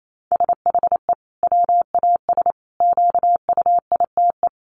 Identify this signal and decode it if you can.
Morse code, heard as a keyed tone, about 28 wpm, 700 Hz. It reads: SHE WAS QUITE
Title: Sentences from Top 1500 Words 28wpm